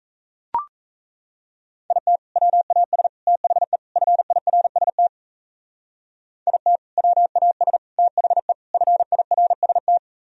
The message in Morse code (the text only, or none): IT WAS THE FIRST IT WAS THE FIRST